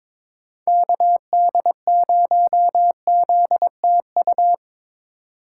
Morse code, heard as a keyed tone, about 21 wpm, 700 Hz. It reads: KD0ZTU
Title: USA Call Signs - 3x Character Spacing - 22wpm